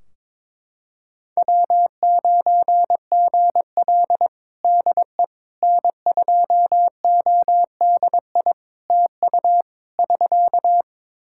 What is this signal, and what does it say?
Morse code, heard as a keyed tone, about 22 wpm, 700 Hz. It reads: W9GL DE N2ODI TU <SK>
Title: Practice QSO - Exchange 4 - Ending - 22wpm